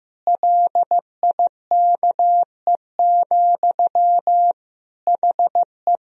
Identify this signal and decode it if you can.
Morse code, heard as a keyed tone, about 15 wpm, 700 Hz. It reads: LIKE, HE
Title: Sentences from Top 1000 Words - Review Part 2 - 15wpm